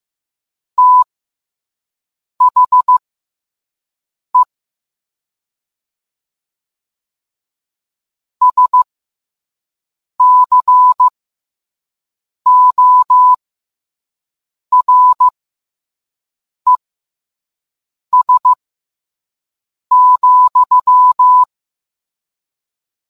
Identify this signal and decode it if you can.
Morse code, heard as a keyed tone, about 15 wpm, spaced as for 5 wpm, 1000 Hz. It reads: THE SCORES,